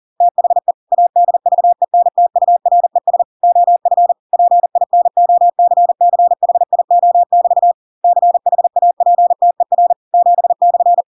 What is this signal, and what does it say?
Morse code, heard as a keyed tone, about 40 wpm, 700 Hz. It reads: THE ADVENTURES OF PINOCCHIO<BT> CHAPTER 7/
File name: Pinocchio / Pinocchio ch7 40 40 ik1hge